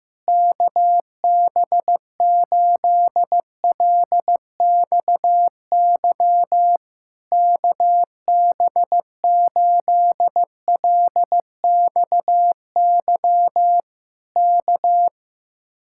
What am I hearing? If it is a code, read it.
Morse code, heard as a keyed tone, about 15 wpm, 700 Hz. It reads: KB8LXY KB8LXY K
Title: Speed-Racing Common QSO Phrases Encore - 15wpm